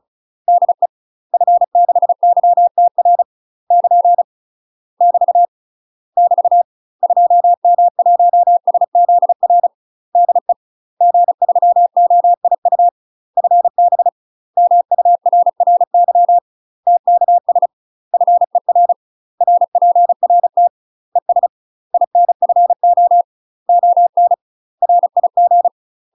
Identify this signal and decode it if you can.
Morse code, heard as a keyed tone, about 34 wpm, 700 Hz. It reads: DE F6YTR <KN> <BT> <BT> 2M1SZR DE G3OIU FB MURRY TKS FER RPRT ES INFO ON RIG
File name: UK MASTER QSOs TRAINING / 35WPM / UK MASTER QSOs 0025